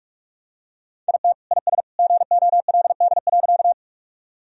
Morse code, heard as a keyed tone, about 43 wpm, 700 Hz. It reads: IT IS GOLD.
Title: Sentences from Top 1000 Words - Review Part 3 - 45wpm